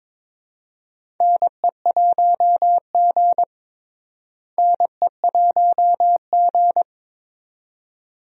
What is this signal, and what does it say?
Morse code, heard as a keyed tone, about 21 wpm, 700 Hz. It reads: NE1G NE1G
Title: USA Call Signs - 4 Characters - Repeated 2x - 22wpm